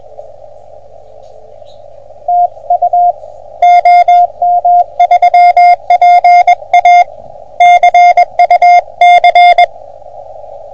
Morse code, heard as a keyed tone, about 21 wpm, 720 Hz. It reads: TU OM3PA CUC
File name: OM3PA PILUP CUC 388